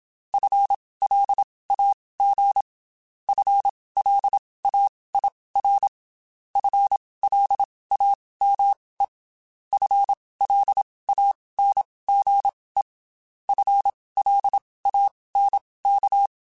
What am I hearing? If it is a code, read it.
Morse code, heard as a keyed tone, about 27 wpm, spaced as for 19 wpm, 800 Hz. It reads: FLAG FLAIR FLAME FLANGE FLANK
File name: Spaced / FL Runs sp mp3s / FL Runs sp @25